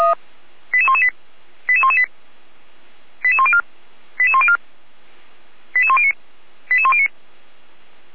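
Five-tone selective call (ZVEI), heard as quick runs of five tones, tones of about 70 ms. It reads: 88118 88118 88104 88104 88109 88109
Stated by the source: ZVEI five-tone selective call received by SDR, archived by Feuerwehr Orlamünde (fire brigade).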